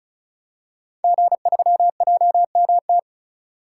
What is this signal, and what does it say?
Morse code, heard as a keyed tone, about 36 wpm, 700 Hz. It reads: G3JMT
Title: Code Groups - 5 Letter:Numbers - 35wpm